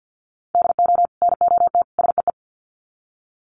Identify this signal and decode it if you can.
Morse code, heard as a keyed tone, about 48 wpm, 700 Hz. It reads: DO NOT SEE
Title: Sets of 3 Words 50wpm